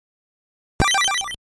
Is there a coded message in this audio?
No.